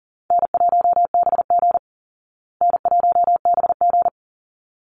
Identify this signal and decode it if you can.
Morse code, heard as a keyed tone, about 40 wpm, 700 Hz. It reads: N1BG N1BG
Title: Easy Call Signs Repeated 2x - 40wpm